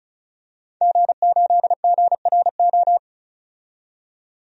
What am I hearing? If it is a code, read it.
Morse code, heard as a keyed tone, about 36 wpm, 700 Hz. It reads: G8GRO
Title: DXCC-100 Call Signs - 5 Characters - 35wpm